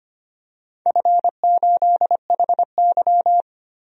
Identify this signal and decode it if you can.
Morse code, heard as a keyed tone, about 25 wpm, 700 Hz. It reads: F8HY
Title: DXCC-100 Call Signs - 4 Characters - Repeated 3x - 25wpm